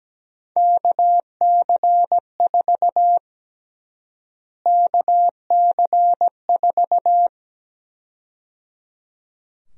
Morse code, heard as a keyed tone, about 17 wpm, 700 Hz. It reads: KC4 KC4
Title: USA Call Sign Prefixes - Repeated 2x - 17wpm